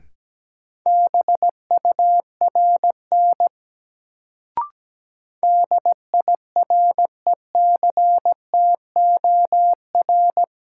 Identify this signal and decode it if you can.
Morse code, heard as a keyed tone, about 17 wpm, 700 Hz. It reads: BURN DIRECTOR
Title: Sentences from Top 1300 Words - New Word Drill - 17wpm